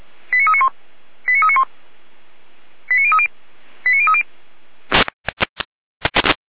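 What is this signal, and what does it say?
Five-tone selective call (ZVEI), heard as quick runs of five tones, tones of about 70 ms. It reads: 89391 89391 89030 89030
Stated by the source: ZVEI five-tone selective call received on SDR.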